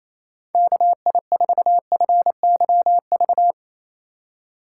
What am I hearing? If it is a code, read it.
Morse code, heard as a keyed tone, about 35 wpm, 700 Hz. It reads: KI4FYV